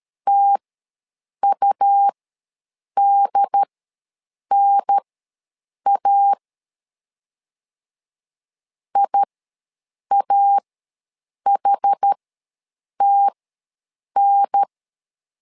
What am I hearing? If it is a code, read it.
Morse code, heard as a keyed tone, about 13 wpm, spaced as for 7 wpm, 800 Hz. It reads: TUDNA IAHTN
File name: LEZIONI / 2 UHDN / 2 UHDN ws 12wpm 1708